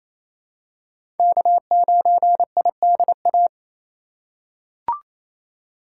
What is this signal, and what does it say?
Morse code, heard as a keyed tone, about 28 wpm, 700 Hz. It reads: K9IDA
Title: USA Call Signs - 5 Characters - 4x Character Spacing - 28wpm